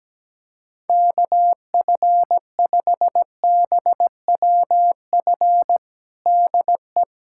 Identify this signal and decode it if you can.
Morse code, heard as a keyed tone, about 17 wpm, 700 Hz. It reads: KF5BWF DE
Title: USA Callsign DE Callsign - 6 Characters - 17wpm